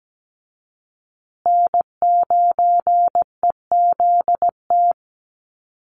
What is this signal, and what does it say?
Morse code, heard as a keyed tone, about 17 wpm, 700 Hz. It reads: N9EZT
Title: USA Call Signs Repeated 3x 17wpm